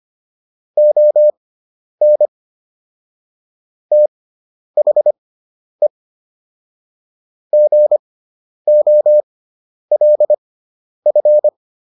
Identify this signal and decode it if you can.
Morse code, heard as a keyed tone, about 25 wpm, spaced as for 10 wpm, 600 Hz. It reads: ON THE GOLF